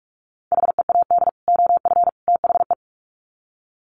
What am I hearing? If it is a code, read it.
Morse code, heard as a keyed tone, about 44 wpm, 700 Hz. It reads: HEAD OF THE
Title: Sets of 3 Words 45wpm